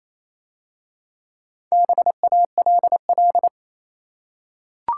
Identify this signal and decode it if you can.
Morse code, heard as a keyed tone, about 27 wpm, 700 Hz. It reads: BALL
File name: Easy Words - Speed-Racing with Hints - 22wpm